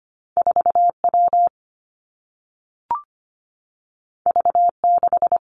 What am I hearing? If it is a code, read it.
Morse code, heard as a keyed tone, about 25 wpm, 700 Hz. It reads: 4 W V 6
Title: Two Challenging Letter-Number Combinations 25wpm